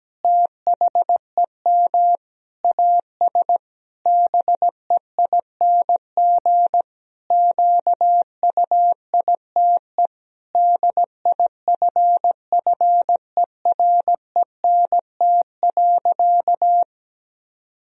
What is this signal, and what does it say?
Morse code, heard as a keyed tone, about 17 wpm, 700 Hz. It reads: THEM AS BEING QUITE DIFFERENT.
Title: Sentences from Top 1000 Words - Review Part 3 - 17wpm